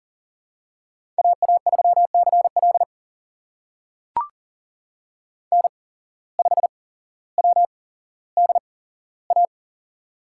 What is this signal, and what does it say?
Morse code, heard as a keyed tone, about 40 wpm, spaced as for 17 wpm, 700 Hz. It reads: AA3CL N5WDA
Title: USA Call Signs - 5 Characters - 8x Character Spacing - 40wpm